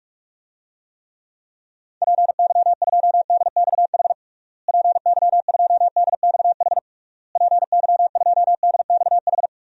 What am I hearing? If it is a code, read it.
Morse code, heard as a keyed tone, about 46 wpm, 700 Hz. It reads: PY2DXH PY2DXH PY2DXH
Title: DXCC-100 Call Signs - 6 Characters - Repeated 3x - 45wpm